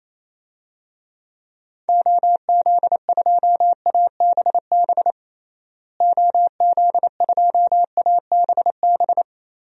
Morse code, heard as a keyed tone, about 28 wpm, 700 Hz. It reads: OZ2ABB OZ2ABB
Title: DXCC-100 Call Signs - 6 Characters - Repeated 3x - 28wpm